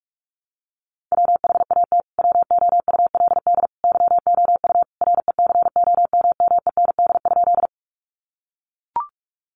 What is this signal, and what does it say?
Morse code, heard as a keyed tone, about 44 wpm, 700 Hz. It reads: WHAT WOULD YOU RECOMMEND?
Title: Sentences from Top 1200 Words 45wpm